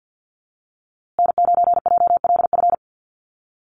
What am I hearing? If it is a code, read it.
Morse code, heard as a keyed tone, about 50 wpm, 700 Hz. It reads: N9JLF
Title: USA Call Signs Repeated 2x 50wpm